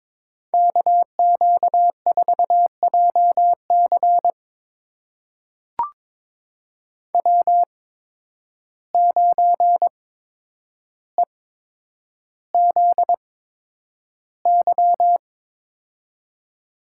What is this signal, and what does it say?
Morse code, heard as a keyed tone, about 22 wpm, spaced as for 9 wpm, 700 Hz. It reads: KQ4JC W9EZY
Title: USA Call Signs - 5 Characters - 8x Character Spacing - 22wpm